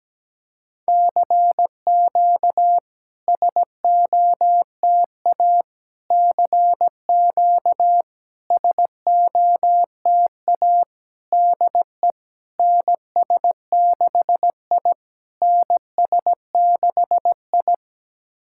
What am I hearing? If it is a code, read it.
Morse code, heard as a keyed tone, about 17 wpm, 700 Hz. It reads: CQ SOTA CQ SOTA DE NS6I NS6I
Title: SOTA QSO Practice 17wpm